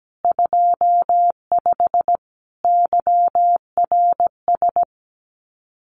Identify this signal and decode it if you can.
Morse code, heard as a keyed tone, about 17 wpm, 700 Hz. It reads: 25 YRS
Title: Common QSO Phrases Encore 17wpm